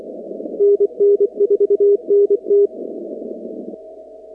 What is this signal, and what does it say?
Morse code, heard as a keyed tone, about 24 wpm, 400 Hz. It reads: NN4NT
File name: contests / 2019-cw-wpx-kt5j-k5pi- -2330 0-44